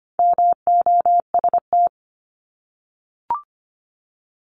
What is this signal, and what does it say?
Morse code, heard as a keyed tone, about 25 wpm, 700 Hz. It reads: MOST
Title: Top 200 Words 25wpm